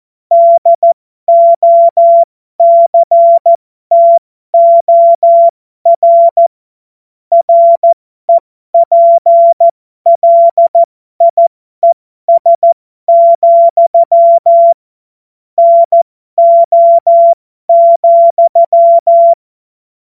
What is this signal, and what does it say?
Morse code, heard as a keyed tone, about 14 wpm, spaced as for 12 wpm, 680 Hz. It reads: DOCTOR REPLIES, NO,